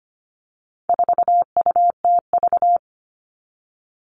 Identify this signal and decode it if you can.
Morse code, heard as a keyed tone, about 25 wpm, 700 Hz. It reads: 4UTV
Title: All Cut Numbers - 4-Digit 25wpm